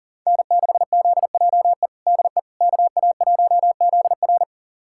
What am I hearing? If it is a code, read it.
Morse code, heard as a keyed tone, about 40 wpm, 700 Hz. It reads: N6ZJE DE KA1ZR
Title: USA Callsign DE Callsign - 5 Characters - 40wpm